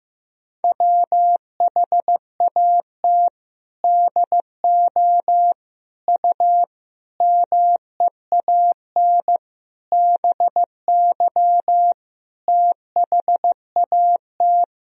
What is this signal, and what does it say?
Morse code, heard as a keyed tone, about 15 wpm, 700 Hz. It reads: WHAT DO U MEAN BY THAT
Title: Sentences from Top 400 Words and Abbreviations 15wpm